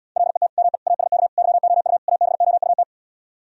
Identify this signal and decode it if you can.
Morse code, heard as a keyed tone, about 77 wpm, 700 Hz. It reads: LET ME ASK YOU ABOUT